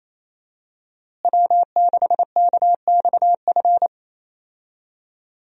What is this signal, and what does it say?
Morse code, heard as a keyed tone, about 28 wpm, 700 Hz. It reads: W6KXF
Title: USA Call Signs - 5 Characters - Repeated 2x - 28wpm